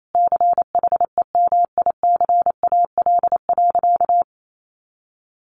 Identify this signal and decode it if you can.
Morse code, heard as a keyed tone, about 28 wpm, 700 Hz. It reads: CHEMICAL.
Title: Sentences from Top 1200 Words 28wpm